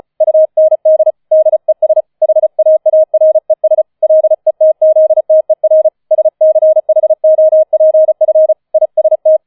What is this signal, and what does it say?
Morse code, heard as a keyed tone, about 34 wpm, 630 Hz. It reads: UND DES HAARES LETZTER SCHOPF IST
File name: 004 Vierter Streich 35wpm